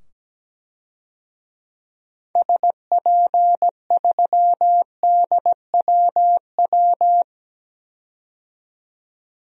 Morse code, heard as a keyed tone, about 17 wpm, 700 Hz. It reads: SP3DWW
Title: DXCC-100 Call Signs - 6 Characters - 17wpm